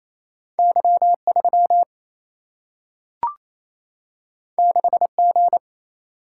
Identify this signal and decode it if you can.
Morse code, heard as a keyed tone, about 28 wpm, 700 Hz. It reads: Y3 6G
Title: Single-Letter Single-Number 28wpm